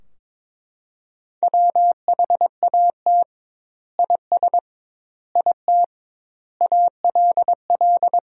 Morse code, heard as a keyed tone, about 22 wpm, 700 Hz. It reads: WHAT IS IT ALL
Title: 2x Word-Spacing - Sentences from Top 100 Words - 22wpm